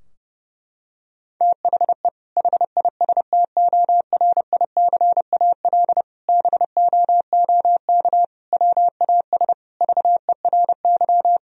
Morse code, heard as a keyed tone, about 32 wpm, 700 Hz. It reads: THE HISTORICAL BOOK WAS VERY
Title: Sentences from Top 1000 Words - Review Part 4 - 30wpm